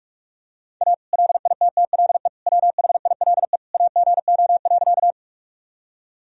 Morse code, heard as a keyed tone, about 42 wpm, 700 Hz. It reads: A LITTLE WHILE AGO.
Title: Sentences from Top 1000 Words - Review Part 3 - 45wpm